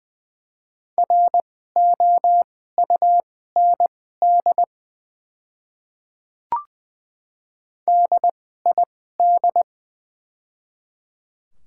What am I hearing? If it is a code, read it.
Morse code, heard as a keyed tone, about 20 wpm, spaced as for 15 wpm, 690 Hz. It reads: ROUND DID
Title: 2x Character Spacing - Top 300 Words - 20wpm